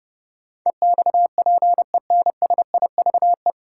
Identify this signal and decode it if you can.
Morse code, heard as a keyed tone, about 30 wpm, 700 Hz. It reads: EXPENSIVE
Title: Sentences from Top 1000 Words - Review Part 6 - New Word Drill - 30wpm